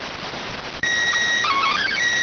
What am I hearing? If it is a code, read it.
SSTV (slow-scan television), PD 290.